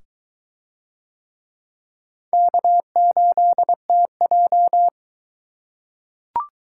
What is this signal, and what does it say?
Morse code, heard as a keyed tone, about 21 wpm, 700 Hz. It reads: K8TJ